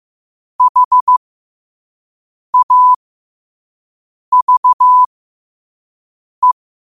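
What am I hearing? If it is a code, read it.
Morse code, heard as a keyed tone, about 15 wpm, spaced as for 5 wpm, 1000 Hz. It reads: HAVE
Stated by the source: Morse code sent at 5 words per minute.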